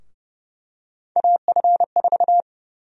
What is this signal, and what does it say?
Morse code, heard as a keyed tone, about 28 wpm, 700 Hz. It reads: AF4
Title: USA Call Sign Prefixes - Repeated 2x - 30wpm